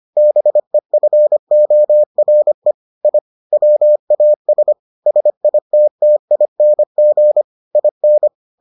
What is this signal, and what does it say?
Morse code, heard as a keyed tone, about 25 wpm, 600 Hz. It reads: BEFORE I WAS SITTING IN